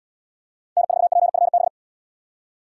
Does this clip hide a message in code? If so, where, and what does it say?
Yes: Morse code from 0.8 to 1.7 s, heard as a keyed tone, about 70 wpm, 700 Hz. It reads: N4PFB